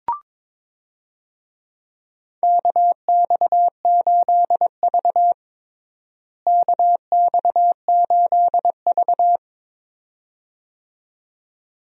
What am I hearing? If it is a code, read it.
Morse code, heard as a keyed tone, about 22 wpm, 700 Hz. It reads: KX8V KX8V
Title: USA Call Signs - 4 Characters - Repeated 2x - 22wpm